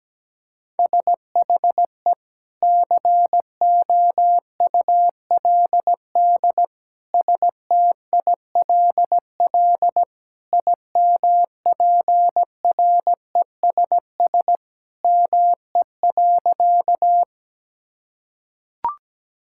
Morse code, heard as a keyed tone, about 17 wpm, 700 Hz. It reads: SHE COULD STILL IMPRESS ME.
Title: Sentences from Top 1600 Words 17wpm